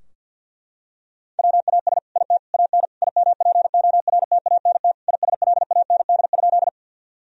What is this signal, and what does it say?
Morse code, heard as a keyed tone, about 50 wpm, 700 Hz. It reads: WAS IT AN IMPORTANT ISLAND?
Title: Sentences from Top 1000 Words - Review Part 3 - 50wpm